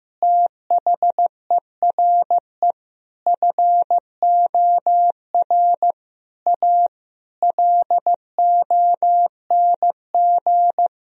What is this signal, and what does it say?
Morse code, heard as a keyed tone, about 15 wpm, 700 Hz. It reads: THERE FOR A LONG
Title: Sentences from Top 1000 Words - Review Part 2 - 15wpm